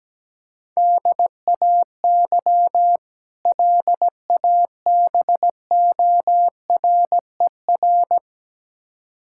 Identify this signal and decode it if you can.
Morse code, heard as a keyed tone, about 17 wpm, 700 Hz. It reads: DAY LABORER
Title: K1USN SST QSO Practice 17wpm